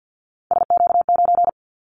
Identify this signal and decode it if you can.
Morse code, heard as a keyed tone, about 50 wpm, 700 Hz. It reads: SQ9
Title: Callsign Prefixes for Top 25 DXCC - 50wpm